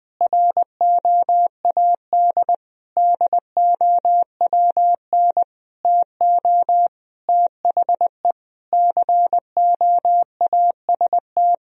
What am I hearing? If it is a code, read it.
Morse code, heard as a keyed tone, about 20 wpm, 700 Hz. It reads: ROAD DOWN TO THE COAST
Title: Sentences from Top 1000 Words - Review Part 4 - 20wpm